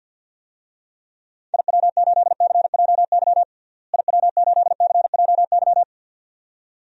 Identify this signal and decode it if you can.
Morse code, heard as a keyed tone, about 47 wpm, 700 Hz. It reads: IW8CJY IW8CJY
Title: DXCC-100 Call Signs - 6 Characters - Repeated 2x - 50wpm